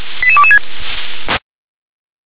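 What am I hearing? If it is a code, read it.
Five-tone selective call (ZVEI), heard as quick runs of five tones, tones of about 80 ms. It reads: 88226
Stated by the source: Five-tone ZVEI selective call 88166 received by SDR.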